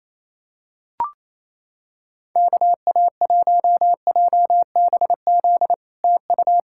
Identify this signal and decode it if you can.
Morse code, heard as a keyed tone, about 28 wpm, 700 Hz. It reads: KA1JBZ TU
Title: SOTA QSO Practice 28wpm